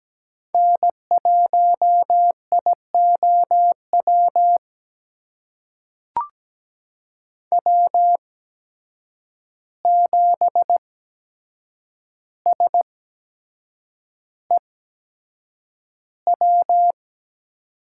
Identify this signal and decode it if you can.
Morse code, heard as a keyed tone, about 17 wpm, spaced as for 7 wpm, 700 Hz. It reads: N1IOW W7SEW
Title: USA Call Signs - 5 Characters - 8x Character Spacing - 17wpm